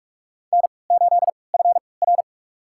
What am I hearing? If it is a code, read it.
Morse code, heard as a keyed tone, about 46 wpm, spaced as for 26 wpm, 700 Hz. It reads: N8FR
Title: USA Call Signs - 4 Characters - 3x Character Spacing - 45wpm